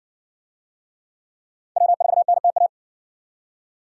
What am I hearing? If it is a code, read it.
Morse code, heard as a keyed tone, about 54 wpm, 700 Hz. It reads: W4NTA